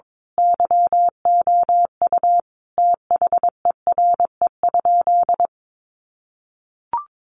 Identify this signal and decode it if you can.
Morse code, heard as a keyed tone, about 22 wpm, 700 Hz. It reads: YOU THERE?
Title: Sentences from Top 100 Words 22wpm